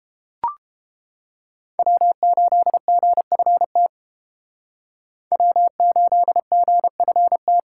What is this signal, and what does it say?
Morse code, heard as a keyed tone, about 30 wpm, 700 Hz. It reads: W8GFT W8GFT